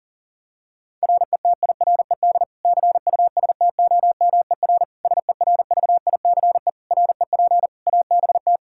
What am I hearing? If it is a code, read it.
Morse code, heard as a keyed tone, about 40 wpm, 700 Hz. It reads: RETIRED CUSTOMER SERVICE REP ABT
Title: Practice QSO - Exchange 3 - More Info - 40wpm